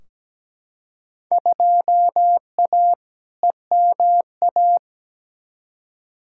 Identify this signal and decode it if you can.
Morse code, heard as a keyed tone, about 17 wpm, 700 Hz. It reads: 2A EMA
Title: Field Day Exchanges - 17wpm